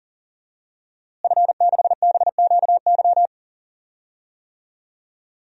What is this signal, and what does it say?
Morse code, heard as a keyed tone, about 37 wpm, 700 Hz. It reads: F6BQY
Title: DXCC-100 Call Signs - 5 Characters - Repeated 3x - 40wpm